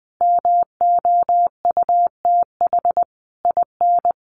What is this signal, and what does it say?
Morse code, heard as a keyed tone, about 20 wpm, 700 Hz. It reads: MOUTH IN
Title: A Sound Of Thunder 20wpm